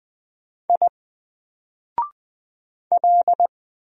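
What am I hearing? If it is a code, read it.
Morse code, heard as a keyed tone, about 20 wpm, 700 Hz. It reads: I L
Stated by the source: Single letter and number Morse drill at 20 wpm, from N4LCD.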